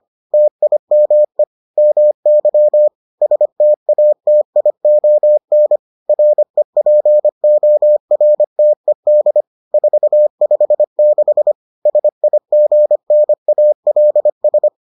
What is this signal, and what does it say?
Morse code, heard as a keyed tone, about 25 wpm, 600 Hz. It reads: TIME MY STATION REPORTED 456 SIGNALS